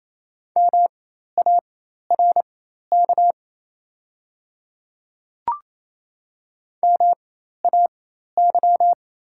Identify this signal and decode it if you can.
Morse code, heard as a keyed tone, about 27 wpm, spaced as for 13 wpm, 690 Hz. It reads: MARK MAY
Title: 4x Character Spacing - Top 300 Words - 28wpm